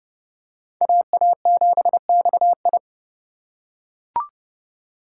Morse code, heard as a keyed tone, about 30 wpm, 700 Hz. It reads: AA7XI